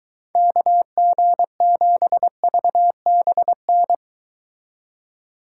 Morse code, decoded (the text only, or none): KG7VBN